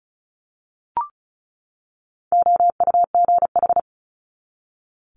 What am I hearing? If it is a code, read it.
Morse code, heard as a keyed tone, about 36 wpm, 700 Hz. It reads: OUGH
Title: Multi-letter Phonemes 35wpm